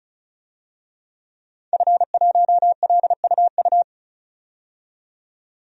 Morse code, heard as a keyed tone, about 34 wpm, 700 Hz. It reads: F1LUU
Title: DXCC-100 Call Signs - 5 Characters - 35wpm